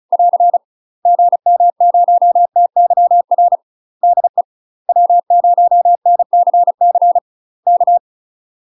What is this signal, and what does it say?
Morse code, heard as a keyed tone, about 36 wpm, 700 Hz. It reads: <AR> GM0TYR DE W0NCC K
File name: UK MASTER QSOs TRAINING / 35WPM / UK MASTER QSOs 0037